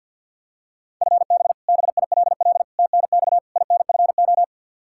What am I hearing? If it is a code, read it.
Morse code, heard as a keyed tone, about 52 wpm, 700 Hz. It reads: FB BILL TNX INFO
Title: Practice QSO - Exchange 3 - More Info - 50wpm